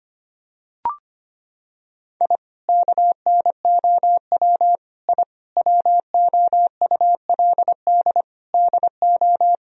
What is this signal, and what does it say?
Morse code, heard as a keyed tone, about 25 wpm, 700 Hz. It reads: I KNOW I WOULD DO
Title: Sentences from Top 1000 Words - Review Part 3 - 25wpm